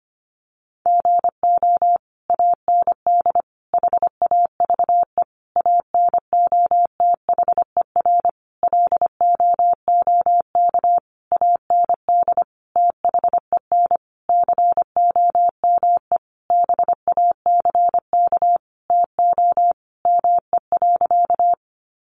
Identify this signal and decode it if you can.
Morse code, heard as a keyed tone, about 25 wpm, 700 Hz. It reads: GO AND HAVE ANOTHER LOOK AND THEN COME BACK TO ME.
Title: Sentences from Top 600 Words 25wpm